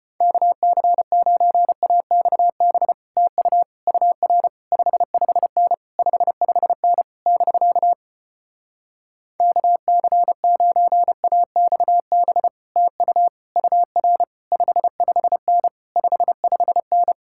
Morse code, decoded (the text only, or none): KC9AXB TU UR 55N 55N <BK> KC9AXB TU UR 55N 55N